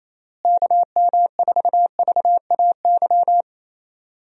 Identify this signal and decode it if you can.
Morse code, heard as a keyed tone, about 27 wpm, 700 Hz. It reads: KM4VAY
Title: USA Call Signs - 6 Characters - Repeated 2x - 28wpm